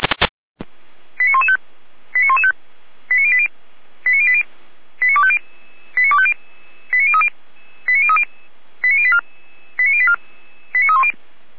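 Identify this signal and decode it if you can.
Five-tone selective call (ZVEI), heard as quick runs of five tones, tones of about 70 ms. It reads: 89115 89115 89080 89080 89250 89250 89030 89030 89084 89084 89310
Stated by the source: ZVEI five-tone selective call received by SDR, from fire brigade Feuerwehr Orlamünde.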